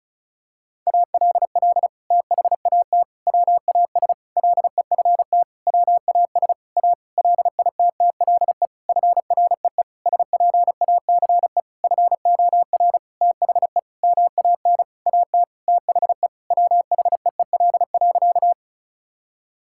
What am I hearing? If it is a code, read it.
Morse code, heard as a keyed tone, about 36 wpm, 700 Hz. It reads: ALL THAT WAS LEFT WAS A LITTLE FREE SPACE FOR THE MAN AT THE WHEEL.
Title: Sentences from Top 1000 Words - Review Part 5 - 35wpm